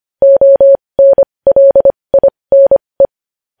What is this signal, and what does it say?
Morse code, heard as a keyed tone, about 25 wpm, spaced as for 20 wpm, 550 Hz. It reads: ONLINE
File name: podcast / News headlines-1x-25x20